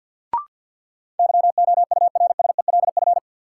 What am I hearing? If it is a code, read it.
Morse code, heard as a keyed tone, about 53 wpm, 700 Hz. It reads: YOURSELF
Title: Sentences from Top 600 Words - New Word Drill - 50wpm